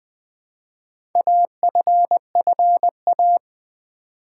Morse code, heard as a keyed tone, about 20 wpm, 700 Hz. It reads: AFFA
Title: Build A Word - 2 Letters At A Time - 6-Letter Words - 20wpm